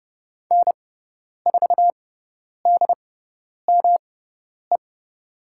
Morse code, heard as a keyed tone, about 30 wpm, spaced as for 10 wpm, 700 Hz. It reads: N4DME
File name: Lesson 18g - Letter m - Call Signs 30wpm